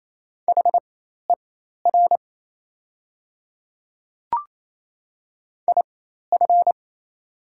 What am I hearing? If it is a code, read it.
Morse code, heard as a keyed tone, about 28 wpm, spaced as for 13 wpm, 690 Hz. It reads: HER IF
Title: 4x Character Spacing - Top 200 Words - 28wpm